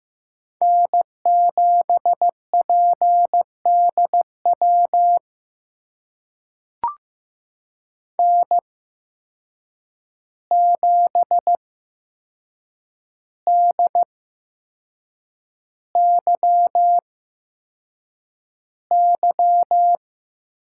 Morse code, heard as a keyed tone, about 15 wpm, spaced as for 6 wpm, 700 Hz. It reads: N7PDW N7DYY
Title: USA Call Signs - 5 Characters - 8x Character Spacing - 15wpm